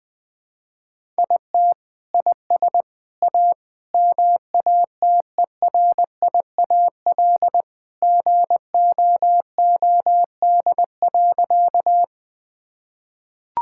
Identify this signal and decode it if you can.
Morse code, heard as a keyed tone, about 20 wpm, 700 Hz. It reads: IT IS A MATERIAL GOOD.
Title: Sentences from Top 1000 Words - Review Part 5 - 20wpm